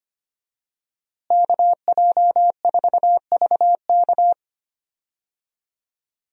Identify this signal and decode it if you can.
Morse code, heard as a keyed tone, about 25 wpm, 700 Hz. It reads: KJ4VK
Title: USA Call Signs - 5 Characters - Repeated 3x - 25wpm